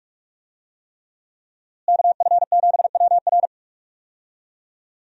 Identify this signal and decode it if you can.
Morse code, heard as a keyed tone, about 41 wpm, 700 Hz. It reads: KF7WR
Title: USA Call Signs - 5 Characters - Repeated 3x - 45wpm